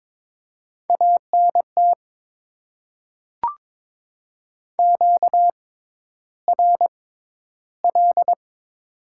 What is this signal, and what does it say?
Morse code, heard as a keyed tone, about 22 wpm, spaced as for 11 wpm, 700 Hz. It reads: ANT QRL
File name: Top 20 QSO Elements - 6x Character Spacing - 22wpm